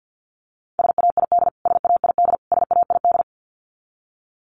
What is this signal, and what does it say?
Morse code, heard as a keyed tone, about 50 wpm, 700 Hz. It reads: SAID SAID SAID
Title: Top 500 Words Repeated 50wpm